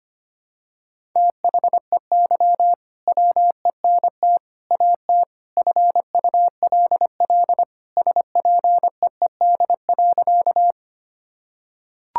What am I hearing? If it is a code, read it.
Morse code, heard as a keyed tone, about 25 wpm, 700 Hz. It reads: THEY WENT AT FULL SPEED.
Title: Sentences from Top 1000 Words - Review Part 5 - 25wpm